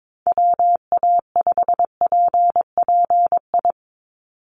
Morse code, heard as a keyed tone, about 22 wpm, 700 Hz. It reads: WA5PPI
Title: USA Call Signs 22wpm